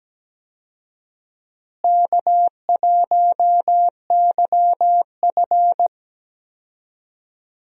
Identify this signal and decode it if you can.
Morse code, heard as a keyed tone, about 17 wpm, 700 Hz. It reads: K1YF
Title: USA Call Signs - 4 Characters - Repeated 3x - 17wpm